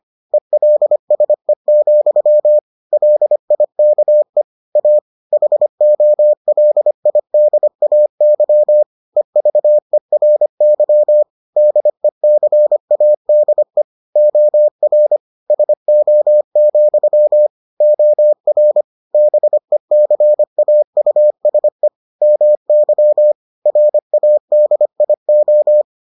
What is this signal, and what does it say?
Morse code, heard as a keyed tone, about 25 wpm, 600 Hz. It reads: ELSE, LIKE A HOLIDAY EVERY DECADE OR SO, OR BECAUSE MY RADIO